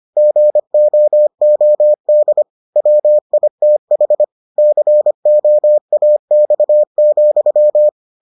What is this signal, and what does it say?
Morse code, heard as a keyed tone, about 25 wpm, 600 Hz. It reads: GOOD WITH COAX,